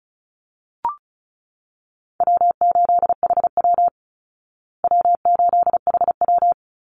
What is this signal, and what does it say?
Morse code, heard as a keyed tone, about 34 wpm, 700 Hz. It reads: W8HW W8HW
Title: Easy Call Signs Repeated 2x - 35wpm